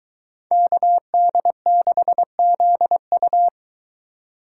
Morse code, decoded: KD6ZU

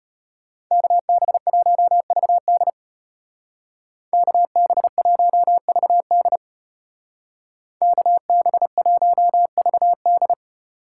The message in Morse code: KB1VD KB1VD KB1VD

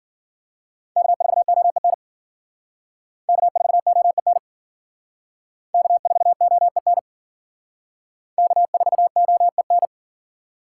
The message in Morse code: K4OEN K4OEN K4OEN K4OEN

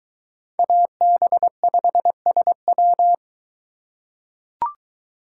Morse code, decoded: AB5SW

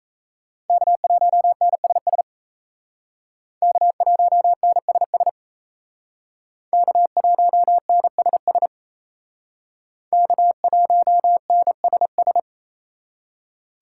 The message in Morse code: K1NSS K1NSS K1NSS K1NSS

